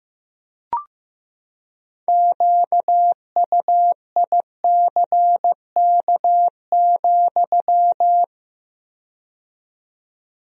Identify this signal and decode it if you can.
Morse code, heard as a keyed tone, about 15 wpm, 700 Hz. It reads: QUICK,
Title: 4x Word-Spacing - Sentences from Top 500 Words - 15wpm